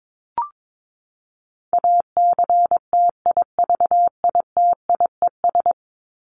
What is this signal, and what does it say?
Morse code, heard as a keyed tone, about 22 wpm, 700 Hz. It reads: ACTIVITIES
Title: ES:S Suffix - Top 100 Words with Base Words 22wpm